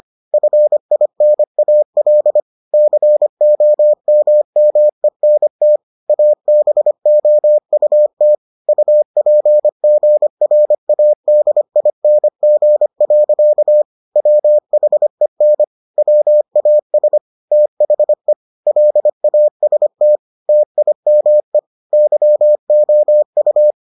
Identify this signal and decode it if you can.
Morse code, heard as a keyed tone, about 25 wpm, 600 Hz. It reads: FINAL COMMENT ABOUT UPGRADING. WHEN WAS THE LAST TIME YOU